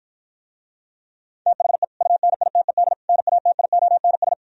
Morse code, heard as a keyed tone, about 57 wpm, 690 Hz. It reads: THE UNITED NATIONS